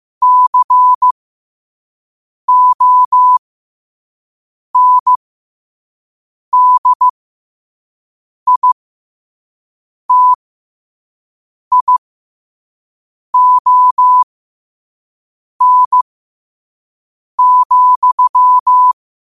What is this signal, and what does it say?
Morse code, heard as a keyed tone, about 15 wpm, spaced as for 5 wpm, 1000 Hz. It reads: CONDITION,
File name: qotdfiles / 5wpm